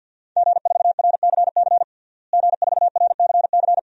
Morse code, heard as a keyed tone, about 50 wpm, 700 Hz. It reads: G4RCC G4RCC
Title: DXCC-100 Call Signs - 5 Characters - Repeated 2x - 50wpm